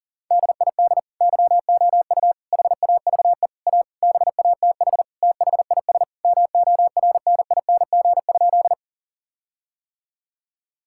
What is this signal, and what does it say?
Morse code, heard as a keyed tone, about 40 wpm, 700 Hz. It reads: DID YOU HAVE A BATH THIS MORNING?
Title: Sentences from Top 1600 Words 40wpm